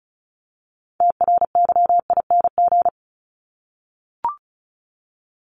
Morse code, heard as a keyed tone, about 34 wpm, 700 Hz. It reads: TRYING
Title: ING Suffix - Top 100 Words with Base Words 35wpm